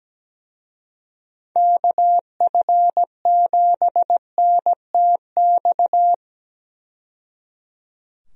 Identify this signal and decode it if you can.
Morse code, heard as a keyed tone, about 17 wpm, 700 Hz. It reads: KF7NTX
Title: USA Call Signs - 6 Characters - 17wpm